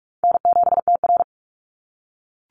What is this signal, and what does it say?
Morse code, heard as a keyed tone, about 46 wpm, 700 Hz. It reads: N7TR
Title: Easy Call Signs - 45wpm